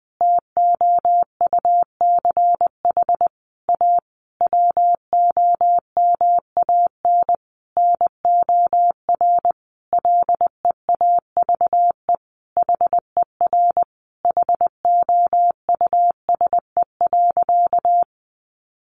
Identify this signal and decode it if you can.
Morse code, heard as a keyed tone, about 20 wpm, 700 Hz. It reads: TOUCH A WOMAN NOR LEAVE HER HOUSE.
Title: Sentences from Top 1500 Words 20wpm